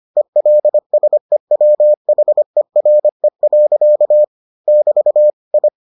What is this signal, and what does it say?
Morse code, heard as a keyed tone, about 25 wpm, 600 Hz. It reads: ELSEWHERE. <BT> I